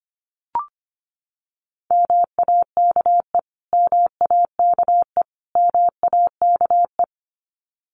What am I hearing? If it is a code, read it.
Morse code, heard as a keyed tone, about 25 wpm, 700 Hz. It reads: MAKE MAKE MAKE
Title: Top 200 Words Repeated 25wpm